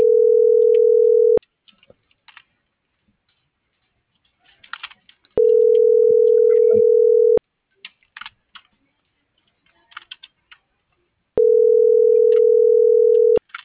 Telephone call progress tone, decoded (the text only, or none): ringback tone